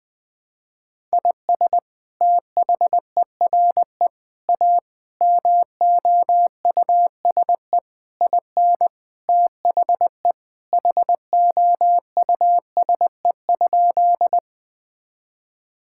Morse code, heard as a keyed tone, about 20 wpm, 700 Hz. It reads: IS THERE A MOUSE IN THE HOUSE?
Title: Sentences from Top 1000 Words - Review Part 5 - 20wpm